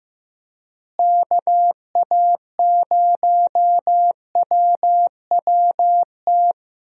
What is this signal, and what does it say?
Morse code, heard as a keyed tone, about 15 wpm, 700 Hz. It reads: KA0WWT
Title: USA Call Signs - 6 Characters - Repeated 3x - 15wpm